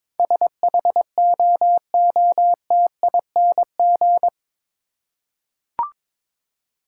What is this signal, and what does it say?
Morse code, heard as a keyed tone, about 22 wpm, 700 Hz. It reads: SHOOTING
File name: practice / 22wpm / Eight Letter Words - 22wpm